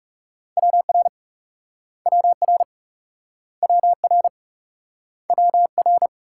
Morse code, heard as a keyed tone, about 38 wpm, 700 Hz. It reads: WR WR WR WR